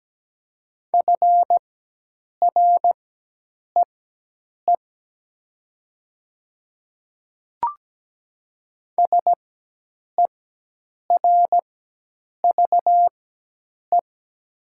Morse code, heard as a keyed tone, about 17 wpm, spaced as for 8 wpm, 700 Hz. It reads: FREE SERVE
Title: 4x Character Spacing - Top 500 Words - 17wpm